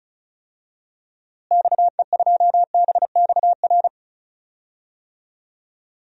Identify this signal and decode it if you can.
Morse code, heard as a keyed tone, about 34 wpm, 700 Hz. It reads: XE2BXR
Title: DXCC-100 Call Signs - 6 Characters - Repeated 3x - 35wpm